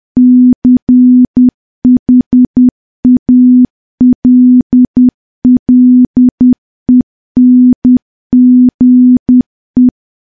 Morse code, heard as a keyed tone, about 10 wpm, 250 Hz. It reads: CHALLENGE
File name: morse / morse-training-10WPM